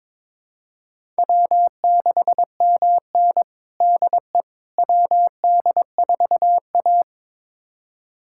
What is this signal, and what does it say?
Morse code, heard as a keyed tone, about 21 wpm, 700 Hz. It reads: W6MN DE WD4A
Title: USA Callsign DE Callsign - 4 Characters - 22wpm